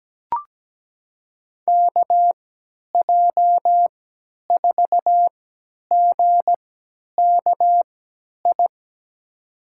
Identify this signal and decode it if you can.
Morse code, heard as a keyed tone, about 17 wpm, spaced as for 10 wpm, 700 Hz. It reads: KJ4GKI
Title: USA Call Signs - 6 Characters - 3x Character Spacing - 17wpm